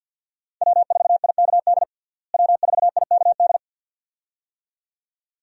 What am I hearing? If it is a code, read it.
Morse code, heard as a keyed tone, about 51 wpm, 700 Hz. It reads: W4IKD W4IKD
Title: USA Call Signs - 5 Characters - Repeated 2x - 50wpm